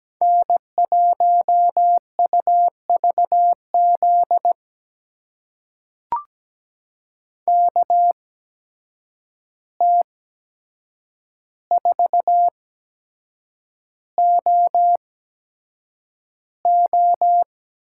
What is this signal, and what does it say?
Morse code, heard as a keyed tone, about 17 wpm, spaced as for 7 wpm, 700 Hz. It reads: N1UVZ KT4OO
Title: USA Call Signs - 5 Characters - 8x Character Spacing - 17wpm